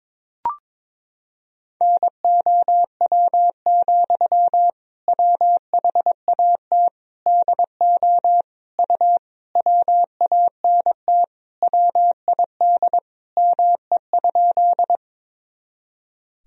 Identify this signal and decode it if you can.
Morse code, heard as a keyed tone, about 22 wpm, 700 Hz. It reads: NOW, WHAT DO U WANT WID ME?
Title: Sentences from Top 200 Words and Abbreviations 22wpm